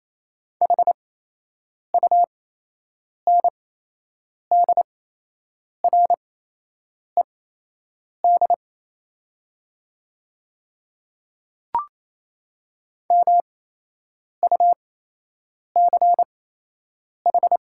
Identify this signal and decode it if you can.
Morse code, heard as a keyed tone, about 28 wpm, spaced as for 8 wpm, 700 Hz. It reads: HUNDRED MUCH
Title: 8x Character Spacing - Top 500 Words - 28wpm